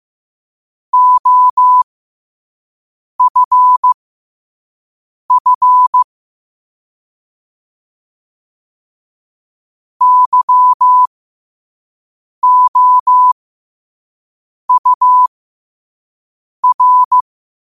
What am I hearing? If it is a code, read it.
Morse code, heard as a keyed tone, about 15 wpm, spaced as for 5 wpm, 1000 Hz. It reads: OFF YOUR
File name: qotdfiles / 5wpm